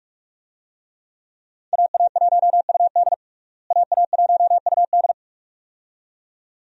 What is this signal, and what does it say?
Morse code, heard as a keyed tone, about 46 wpm, 700 Hz. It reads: AA1UD AA1UD
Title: USA Call Signs - 5 Characters - Repeated 2x - 45wpm